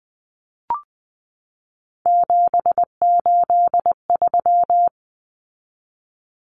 Morse code, heard as a keyed tone, about 20 wpm, 700 Hz. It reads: 783
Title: Common Cut Numbers - 3-Digit 20wpm